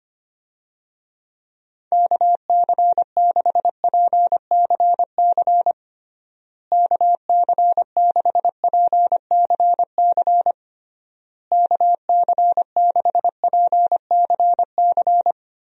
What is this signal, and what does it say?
Morse code, heard as a keyed tone, about 25 wpm, 700 Hz. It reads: KC6PCC KC6PCC KC6PCC
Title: USA Call Signs - 6 Characters - Repeated 3x - 25wpm